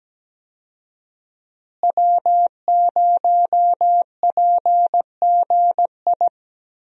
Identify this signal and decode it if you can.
Morse code, heard as a keyed tone, about 17 wpm, 700 Hz. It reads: W0PGI
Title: USA Call Signs - 5 Characters - Repeated 3x - 17wpm